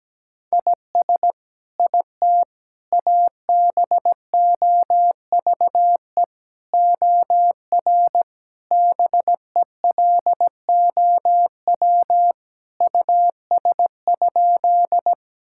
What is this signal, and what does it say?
Morse code, heard as a keyed tone, about 17 wpm, 700 Hz. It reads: IS IT ABOVE OR BELOW US?
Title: Sentences from Top 1000 Words - Review Part 5 - 17wpm